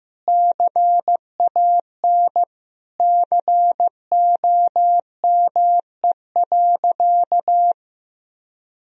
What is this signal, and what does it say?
Morse code, heard as a keyed tone, about 15 wpm, 700 Hz. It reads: CAN COME.
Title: Sentences from Top 1000 Words - Review Part 1 - 15wpm